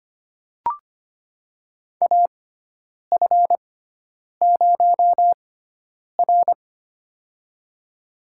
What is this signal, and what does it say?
Morse code, heard as a keyed tone, about 25 wpm, spaced as for 9 wpm, 700 Hz. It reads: AF0R
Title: USA Call Signs - 4 Characters - 6x Character Spacing - 25wpm